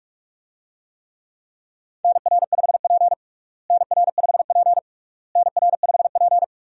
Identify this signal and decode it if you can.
Morse code, heard as a keyed tone, about 42 wpm, 700 Hz. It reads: NR5P NR5P NR5P
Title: USA Call Signs - 4 Characters - Repeated 3x - 45wpm